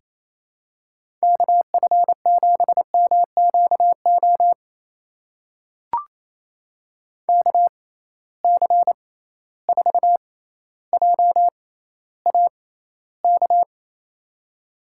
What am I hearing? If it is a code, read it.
Morse code, heard as a keyed tone, about 28 wpm, spaced as for 14 wpm, 700 Hz. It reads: KF7MQO KC4JAK
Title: USA Call Signs - 6 Characters - 6x Character Spacing - 28wpm